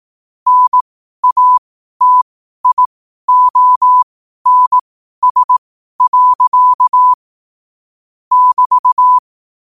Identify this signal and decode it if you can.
Morse code, heard as a keyed tone, about 18 wpm, spaced as for 12 wpm, 1000 Hz. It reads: NATIONS. <BT>
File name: 12wpm